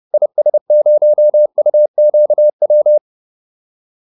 Morse code, heard as a keyed tone, about 30 wpm, 600 Hz. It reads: IS0UQW